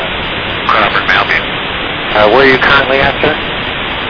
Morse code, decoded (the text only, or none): none